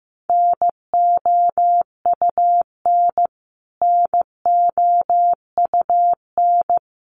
Morse code, heard as a keyed tone, about 15 wpm, 700 Hz. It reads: NOUN NOUN
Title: Top 500 Words Repeated 15wpm